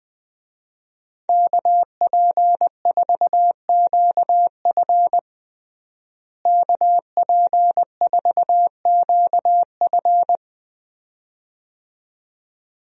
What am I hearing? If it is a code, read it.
Morse code, heard as a keyed tone, about 20 wpm, 700 Hz. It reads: KP4QF KP4QF
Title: DXCC-100 Call Signs - 5 Characters - Repeated 2x - 20wpm